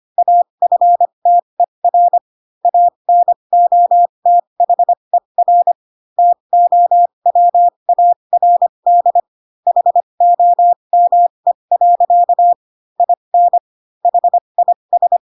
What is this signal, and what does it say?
Morse code, heard as a keyed tone, about 25 wpm, spaced as for 22 wpm, 700 Hz. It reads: AFTER ANOTHER TOWARD HOME. IN HIS
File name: Pinocchio / Pinocchio ch4 18 25 ik1hge